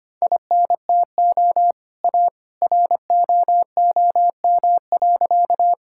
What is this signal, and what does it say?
Morse code, heard as a keyed tone, about 25 wpm, 700 Hz. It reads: INTO A ROOM.
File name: practice / 25wpm / Sentences with 2-Word Phrases - 25wpm